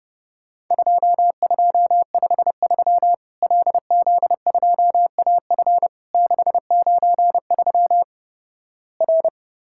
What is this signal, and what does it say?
Morse code, heard as a keyed tone, about 32 wpm, 700 Hz. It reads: 2253 LZ2AF 693 R
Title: Worked All Europe DX Contest - QTC Exchange - 30wpm